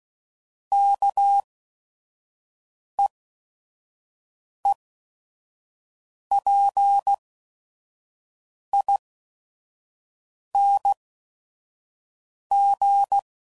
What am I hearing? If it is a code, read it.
Morse code, heard as a keyed tone, about 16 wpm, spaced as for 5 wpm, 800 Hz. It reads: KEEPING
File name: Suffixes / Special ING's / VerySlowToFaster ING's